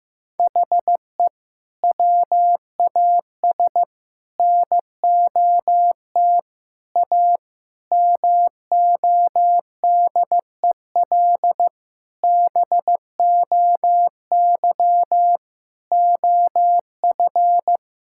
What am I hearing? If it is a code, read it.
Morse code, heard as a keyed tone, about 16 wpm, 700 Hz. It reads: HE WAS NOT A MODEL BOY OF
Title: Sentences from Top 1000 Words - Review Part 5 - 15wpm